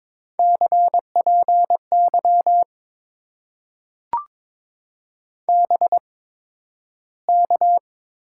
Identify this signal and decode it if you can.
Morse code, heard as a keyed tone, about 22 wpm, spaced as for 12 wpm, 700 Hz. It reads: CPY BK
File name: Top 90 QSO Elements - 8x Character Spacing - 22wpm